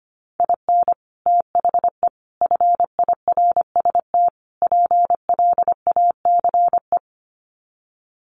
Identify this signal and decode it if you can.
Morse code, heard as a keyed tone, about 25 wpm, 700 Hz. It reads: IN THE FIRST PLACE
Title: Sets of 4 Words 25wpm